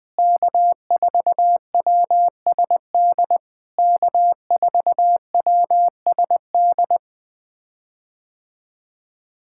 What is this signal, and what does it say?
Morse code, heard as a keyed tone, about 20 wpm, 700 Hz. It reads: K4WSD K4WSD
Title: POTA QSO Practice - 20wpm